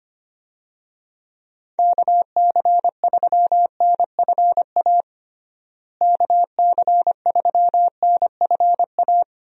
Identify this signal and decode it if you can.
Morse code, heard as a keyed tone, about 25 wpm, 700 Hz. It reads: KC3NFA KC3NFA
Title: USA Call Signs - 6 Characters - Repeated 2x - 25wpm